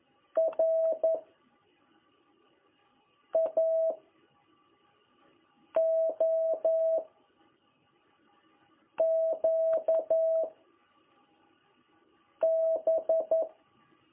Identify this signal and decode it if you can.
Morse code, heard as a keyed tone, about 11 wpm, 660 Hz. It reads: R A O Q B
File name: jh8cbh / 2021cw